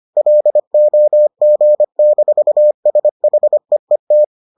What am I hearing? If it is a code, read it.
Morse code, heard as a keyed tone, about 25 wpm, 600 Hz. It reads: LOG-SHEET